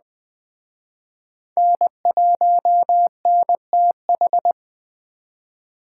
Morse code, heard as a keyed tone, about 20 wpm, 700 Hz. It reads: N1NTH